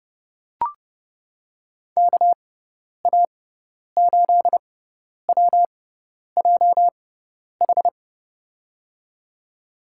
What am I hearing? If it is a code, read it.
Morse code, heard as a keyed tone, about 30 wpm, spaced as for 10 wpm, 700 Hz. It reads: KA8WJH
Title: USA Call Signs - 6x Character Spacing - 30wpm